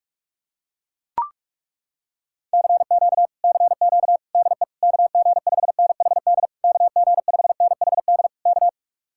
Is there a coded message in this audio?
Yes: Morse code from 2.5 to 8.7 s, heard as a keyed tone, about 42 wpm, 700 Hz. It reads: CQ CQ DE KG5NHD KG5NHD K